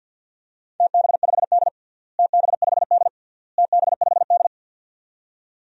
Morse code, heard as a keyed tone, about 48 wpm, 700 Hz. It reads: TB5D TB5D TB5D
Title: DXCC-100 Call Signs - 4 Characters - Repeated 3x - 50wpm